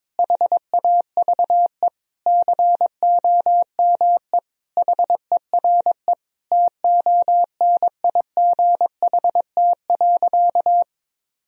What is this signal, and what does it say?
Morse code, heard as a keyed tone, about 22 wpm, 700 Hz. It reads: HAVE COME HERE TONIGHT.
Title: Sentences from Top 1000 Words - Review Part 3 - 22wpm